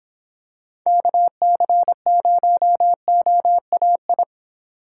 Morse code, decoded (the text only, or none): KC0OAI